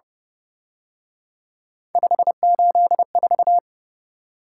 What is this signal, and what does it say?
Morse code, heard as a keyed tone, about 29 wpm, 700 Hz. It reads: 584
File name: Speed-Racing - Three Letter-Number Combinations - 25wpm